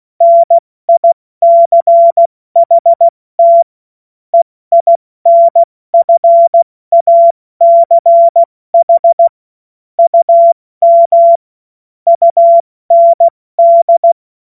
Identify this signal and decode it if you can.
Morse code, heard as a keyed tone, about 16 wpm, 680 Hz. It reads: NICHT EINFACH UM UND
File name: Morsekurs / Witze / Witz 16. Dezember 2025 70 F80